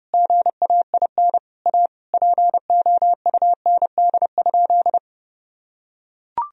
Morse code, heard as a keyed tone, about 30 wpm, 700 Hz. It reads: GAIN A POUND?
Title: Sentences from Top 1000 Words - Review Part 2 - 30wpm